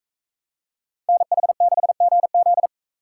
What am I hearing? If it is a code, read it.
Morse code, heard as a keyed tone, about 44 wpm, 700 Hz. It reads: NH6GZ